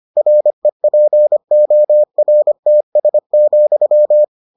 Morse code, heard as a keyed tone, about 25 wpm, 600 Hz. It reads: REPORTS,